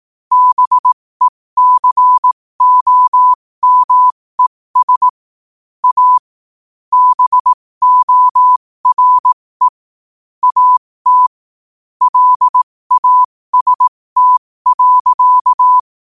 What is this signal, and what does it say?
Morse code, heard as a keyed tone, about 18 wpm, spaced as for 15 wpm, 1000 Hz. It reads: BECOMES A BORE AT LAST.
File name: qotdfiles / 15wpm